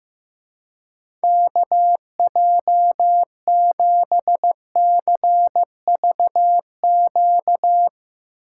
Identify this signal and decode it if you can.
Morse code, heard as a keyed tone, about 15 wpm, 700 Hz. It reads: KJ7CVQ
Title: Speed Racing - USA 2x3 Call Signs 15wpm